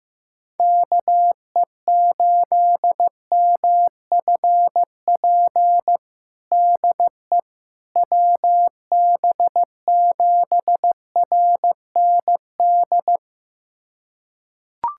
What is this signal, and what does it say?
Morse code, heard as a keyed tone, about 15 wpm, 700 Hz. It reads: KE8MFP DE WB7RND
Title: USA Callsign DE Callsign - 6 Characters - 15wpm